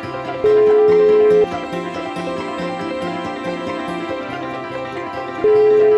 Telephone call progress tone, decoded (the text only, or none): ringback tone